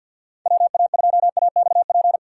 Morse code, heard as a keyed tone, about 62 wpm, 700 Hz. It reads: WA2AXP
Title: Speed Racing - USA Call Signs 50wpm